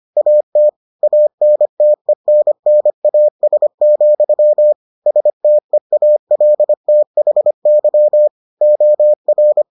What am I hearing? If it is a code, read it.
Morse code, heard as a keyed tone, about 25 wpm, 600 Hz. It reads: AT ANTENNAS, STEALTHY OR